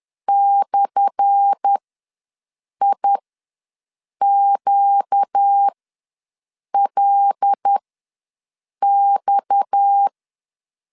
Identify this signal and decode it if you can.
Morse code, heard as a keyed tone, about 10 wpm, spaced as for 6 wpm, 800 Hz. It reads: /IQLX
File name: LEZIONI / 10 SIGNS / 10A SLASH ws 10wpm 4121